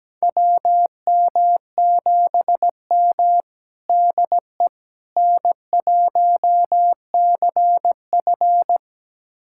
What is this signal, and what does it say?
Morse code, heard as a keyed tone, about 17 wpm, 700 Hz. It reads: WM7M DE N1CF
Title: USA Callsign DE Callsign - 4 Characters - 17wpm